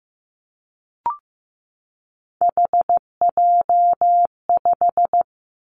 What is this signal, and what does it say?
Morse code, heard as a keyed tone, about 15 wpm, 700 Hz. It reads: HJ5
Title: Callsign Prefixes for Top 100 DXCC - 15wpm